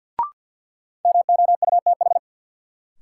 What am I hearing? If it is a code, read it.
Morse code, heard as a keyed tone, about 52 wpm, 700 Hz. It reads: MOUTH
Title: Sentences from Top 1500 Words - New Word Drill Alt - 50wpm